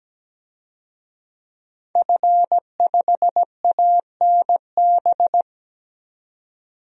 Morse code, heard as a keyed tone, about 17 wpm, 700 Hz. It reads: F5ANB
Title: DXCC-100 Call Signs - 5 Characters - 17wpm